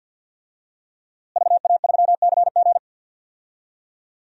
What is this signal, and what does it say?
Morse code, heard as a keyed tone, about 53 wpm, 700 Hz. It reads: VA3CG
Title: DXCC-100 Call Signs - 5 Characters - Repeated 2x - 50wpm